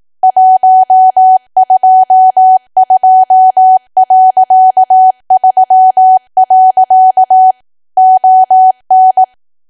Morse code, heard as a keyed tone, about 18 wpm, 750 Hz. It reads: 122.3. ON
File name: cwa / W1AW-Level2 / 18WPM